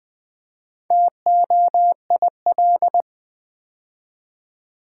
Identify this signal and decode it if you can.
Morse code, heard as a keyed tone, about 20 wpm, 690 Hz. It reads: TOIL
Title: Build A Word - 2 Letters At A Time - 6-Letter Words - 20wpm